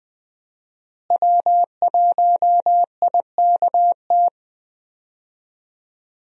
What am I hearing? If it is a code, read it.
Morse code, heard as a keyed tone, about 20 wpm, 700 Hz. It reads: W1IKT
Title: USA Call Signs - 5 Characters - Repeated 2x - 20wpm